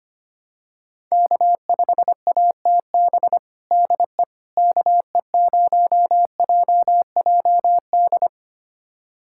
Morse code, heard as a keyed tone, about 25 wpm, 700 Hz. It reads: K5ATB DE KE0JJD
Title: USA Callsign DE Callsign 25wpm